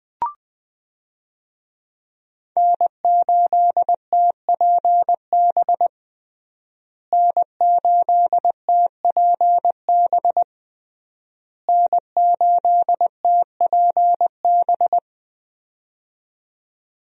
Morse code, heard as a keyed tone, about 21 wpm, 700 Hz. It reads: N8TPB N8TPB N8TPB
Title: USA Call Signs - 5 Characters - Repeated 3x - 20wpm